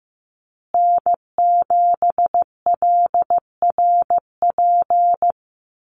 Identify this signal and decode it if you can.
Morse code, heard as a keyed tone, about 15 wpm, 700 Hz. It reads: N7LRP
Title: USA Call Signs Repeated 2x 15wpm